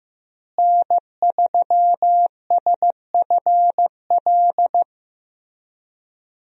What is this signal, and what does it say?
Morse code, heard as a keyed tone, about 15 wpm, 700 Hz. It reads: N3SFL
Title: USA Call Signs - 5 Characters - Repeated 3x - 15wpm